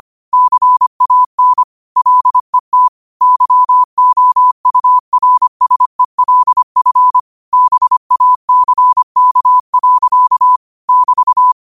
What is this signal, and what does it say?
Morse code, heard as a keyed tone, about 25 wpm, 1000 Hz. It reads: CAN LET YOURSELF BACK. <BT>
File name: qotdfiles / 25wpm